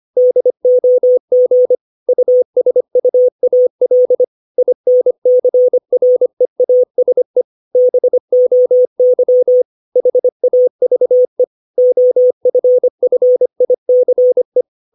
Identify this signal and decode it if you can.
Morse code, heard as a keyed tone, about 25 wpm, 500 Hz. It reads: DOG USUAL INCREASE BOY HAVE OFFICE
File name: 632 words / 632 Words 25WPM